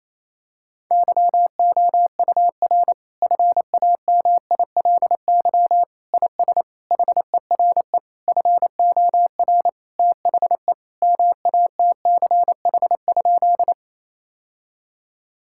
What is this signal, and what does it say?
Morse code, heard as a keyed tone, about 28 wpm, 700 Hz. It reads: YOUR FAMILY IS HERE FOR THE MATCH?
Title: Sentences from Top 1600 Words 28wpm